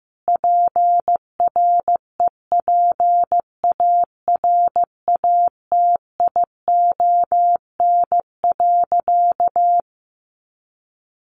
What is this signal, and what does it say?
Morse code, heard as a keyed tone, about 15 wpm, 700 Hz. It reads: PREPARATION.
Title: Sentences from Top 1400 Words 15wpm